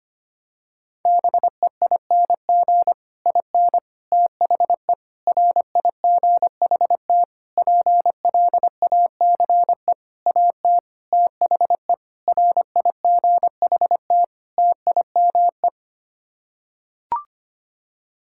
Morse code, heard as a keyed tone, about 25 wpm, 700 Hz. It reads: BEING IN THE RIGHT PLACE AT THE RIGHT TIME
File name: Sets of 9 Words - 25wpm